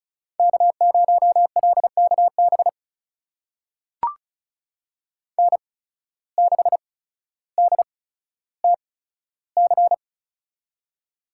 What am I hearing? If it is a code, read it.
Morse code, heard as a keyed tone, about 36 wpm, spaced as for 15 wpm, 700 Hz. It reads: K0LKB N6DTC
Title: USA Call Signs - 5 Characters - 8x Character Spacing - 35wpm